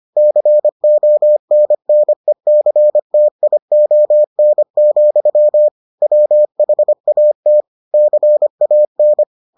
Morse code, heard as a keyed tone, about 25 wpm, 600 Hz. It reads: CONNECTION, WHAT CAN